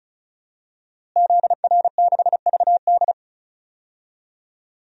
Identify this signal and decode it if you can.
Morse code, heard as a keyed tone, about 36 wpm, 700 Hz. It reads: ZR6VD
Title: DXCC-100 Call Signs - 5 Characters - Repeated 2x - 35wpm